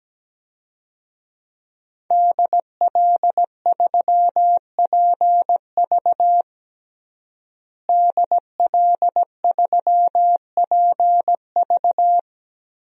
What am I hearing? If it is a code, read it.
Morse code, heard as a keyed tone, about 17 wpm, 700 Hz. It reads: DL3PV DL3PV
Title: DXCC-100 Call Signs - 5 Characters - Repeated 2x - 17wpm